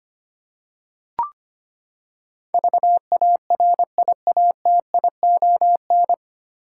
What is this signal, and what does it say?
Morse code, heard as a keyed tone, about 25 wpm, 700 Hz. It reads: VARIATION
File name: Nine Letter Words - 25wpm